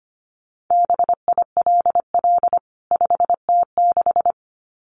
Morse code, heard as a keyed tone, about 25 wpm, 700 Hz. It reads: BILL 5T6
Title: CWT Exchange 25wpm